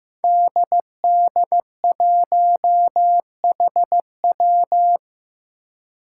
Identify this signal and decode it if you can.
Morse code, heard as a keyed tone, about 15 wpm, 700 Hz. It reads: DD1HW
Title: Five Challenging Letter-Number Combinations 15wpm